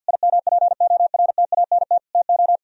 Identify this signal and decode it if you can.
Morse code, heard as a keyed tone, about 64 wpm, 700 Hz. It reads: IMPORTANT TO